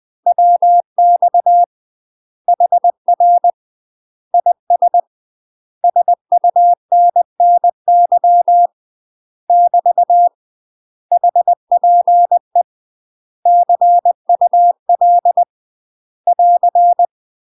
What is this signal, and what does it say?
Morse code, heard as a keyed tone, about 20 wpm, 700 Hz. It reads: WX HR IS SUNNY <BT> HPE CUL <AR>
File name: UK MASTER QSOs TRAINING / 20WPM / UK MASTER QSOs 0067